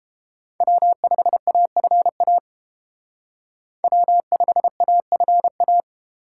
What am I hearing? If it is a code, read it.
Morse code, heard as a keyed tone, about 29 wpm, 700 Hz. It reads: W5AFA W5AFA